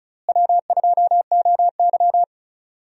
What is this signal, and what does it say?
Morse code, heard as a keyed tone, about 36 wpm, 700 Hz. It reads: W2OY
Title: USA Call Signs - 4 Characters - Repeated 3x - 35wpm